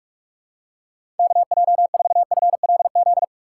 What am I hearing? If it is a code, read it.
Morse code, heard as a keyed tone, about 46 wpm, 700 Hz. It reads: KJ4FLZ
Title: USA Call Signs - 6 Characters - 4x Character Spacing - 45wpm